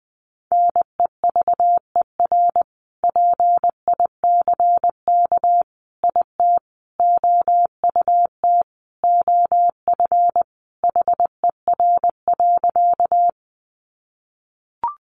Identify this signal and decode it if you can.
Morse code, heard as a keyed tone, about 20 wpm, 700 Hz. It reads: NEVER PICK IT OUT OF HER.
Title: Sentences from Top 1000 Words 20wpm